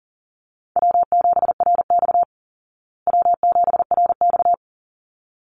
Morse code, heard as a keyed tone, about 40 wpm, 700 Hz. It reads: W7RX W7RX
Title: Easy Call Signs Repeated 2x - 40wpm